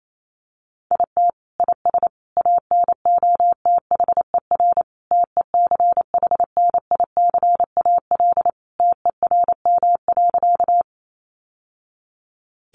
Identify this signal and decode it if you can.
Morse code, heard as a keyed tone, about 28 wpm, 700 Hz. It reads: IT IS ANOTHER TECHNICAL TERM.
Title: Sentences from Top 1100 Words 28wpm